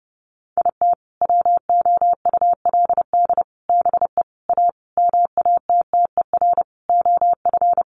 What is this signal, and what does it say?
Morse code, heard as a keyed tone, about 30 wpm, 700 Hz. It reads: IT WOULD BE A MATTER OF
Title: Sentences from Top 1400 Words 30wpm